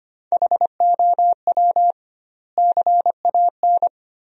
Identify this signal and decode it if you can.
Morse code, heard as a keyed tone, about 25 wpm, 700 Hz. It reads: HOW CAN
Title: 2x Word-Spacing - Sentences from Top 200 Words - 25wpm